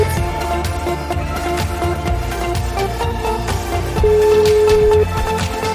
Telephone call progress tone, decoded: ringback tone

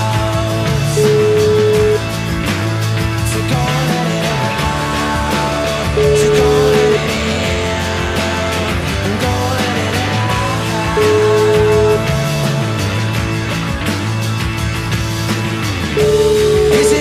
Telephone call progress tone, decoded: ringback tone

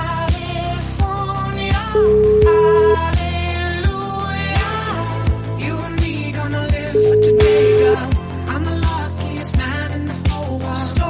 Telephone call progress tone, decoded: ringback tone

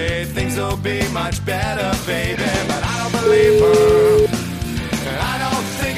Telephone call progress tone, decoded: ringback tone

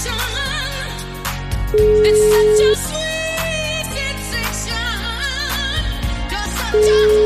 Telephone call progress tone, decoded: ringback tone

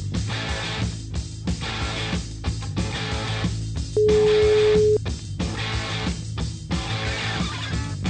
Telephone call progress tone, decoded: ringback tone

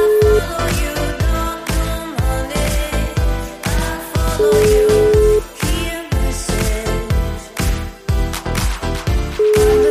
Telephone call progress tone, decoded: ringback tone

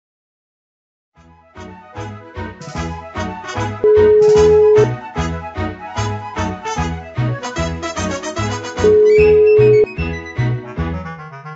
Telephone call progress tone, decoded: ringback tone

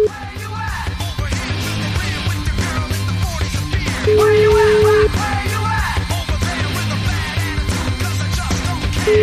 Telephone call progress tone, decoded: ringback tone

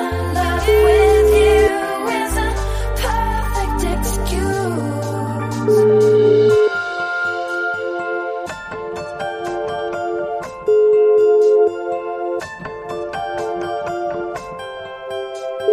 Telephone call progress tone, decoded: ringback tone

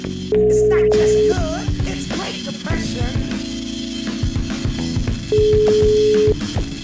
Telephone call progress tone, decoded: ringback tone

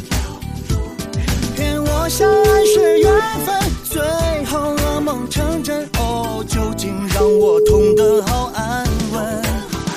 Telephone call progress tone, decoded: ringback tone

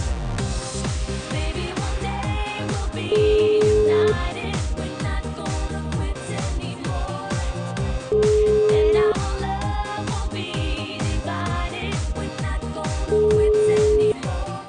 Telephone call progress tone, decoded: ringback tone

